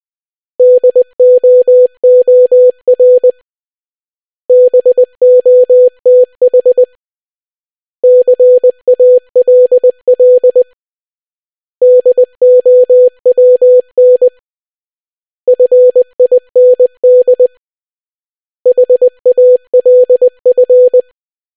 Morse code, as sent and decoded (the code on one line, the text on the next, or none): -.. --- --- .-. / -... --- - .... / -.-. .- .-.. .-.. / -.. --- .-- -. / ..-. .. -. -.. / .... .- .-.. ..-.
DOOR BOTH CALL DOWN FIND HALF